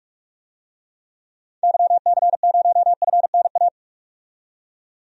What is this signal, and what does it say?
Morse code, heard as a keyed tone, about 47 wpm, 700 Hz. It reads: YC0FNA
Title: DXCC-100 Call Signs - 6 Characters - 45wpm